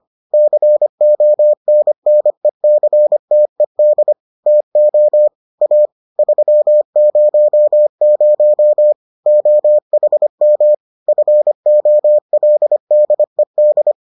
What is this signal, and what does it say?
Morse code, heard as a keyed tone, about 25 wpm, 600 Hz. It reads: CONNECTED TO A 300 OHM FOLDED